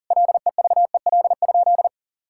Morse code, heard as a keyed tone, about 40 wpm, 700 Hz. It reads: LEVEL?
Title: Sentences from Top 1000 Words - Review Part 5 - 40wpm